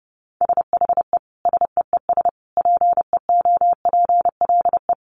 Morse code, heard as a keyed tone, about 30 wpm, 700 Hz. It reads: SHE SEES PEOPLE
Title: Easy Phrases 30wpm